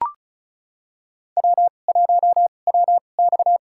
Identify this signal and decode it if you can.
Morse code, heard as a keyed tone, about 34 wpm, spaced as for 25 wpm, 700 Hz. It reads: W1WX
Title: USA Call Signs - 4 Characters - 2x Character Spacing - 35wpm